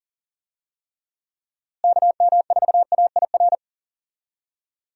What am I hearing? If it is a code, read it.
Morse code, heard as a keyed tone, about 40 wpm, 700 Hz. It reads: KM4AIR
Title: USA Call Signs - 6 Characters - Repeated 2x - 40wpm